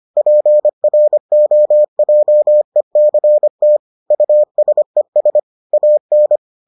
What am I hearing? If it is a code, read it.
Morse code, heard as a keyed tone, about 25 wpm, 600 Hz. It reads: PROJECT USES AN